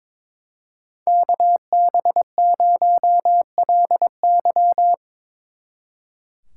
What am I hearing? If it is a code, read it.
Morse code, heard as a keyed tone, about 23 wpm, 700 Hz. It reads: KB0LY
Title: Speed Racing - USA 2x2 Call Signs 22wpm